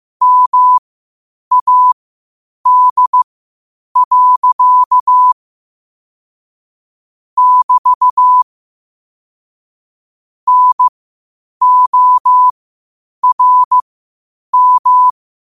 Morse code, heard as a keyed tone, about 15 wpm, spaced as for 8 wpm, 1000 Hz. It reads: MAD. <BT> NORM
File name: qotdfiles / 8wpm